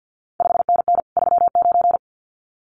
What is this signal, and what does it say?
Morse code, heard as a keyed tone, about 50 wpm, 700 Hz. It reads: 5NN 39
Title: CQ WW DX Contest Exchange - 50wpm